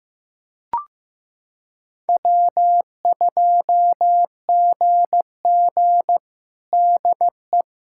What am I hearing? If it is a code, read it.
Morse code, heard as a keyed tone, about 15 wpm, 700 Hz. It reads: W2GG DE
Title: USA Callsign DE Callsign - 4 Characters - 15wpm